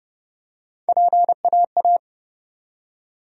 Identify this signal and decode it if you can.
Morse code, heard as a keyed tone, about 30 wpm, 700 Hz. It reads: PAA
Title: USA Call Sign Postfixes - 3 Letters 30wpm